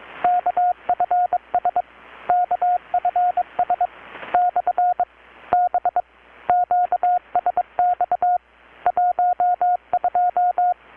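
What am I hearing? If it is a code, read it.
Morse code, heard as a keyed tone, about 21 wpm, 720 Hz, with radio noise. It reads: KFS KFS / B QSX 12
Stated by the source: Morse from KPH's Night of Nights XX maritime radiotelegraphy event, received in Chiapas, Mexico about 3,800 km away on a Kenwood R-600 with a grounded 10 m long-wire antenna.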